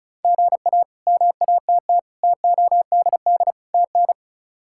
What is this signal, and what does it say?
Morse code, heard as a keyed tone, about 34 wpm, 700 Hz. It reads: GA MATT TODD TN
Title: K1USN SST QSO Practice - 35wpm